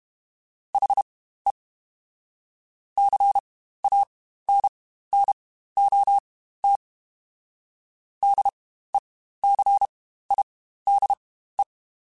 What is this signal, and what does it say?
Morse code, heard as a keyed tone, about 32 wpm, spaced as for 15 wpm, 800 Hz. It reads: HE CANNOT DECIDE
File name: Spaced / Chuckles Spaced-mp3s / HalfFull sp @30